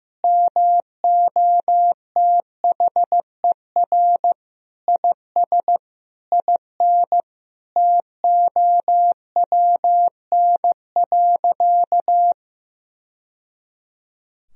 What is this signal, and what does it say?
Morse code, heard as a keyed tone, about 15 wpm, 700 Hz. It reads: MOTHER IS IN TOWN.
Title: Sentences from Top 1000 Words - Review Part 2 - 15wpm